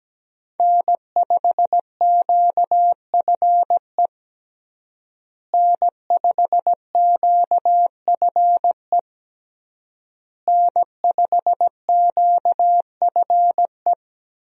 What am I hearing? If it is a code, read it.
Morse code, heard as a keyed tone, about 17 wpm, 700 Hz. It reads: N5QFE N5QFE N5QFE
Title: USA Call Signs - 5 Characters - Repeated 3x - 17wpm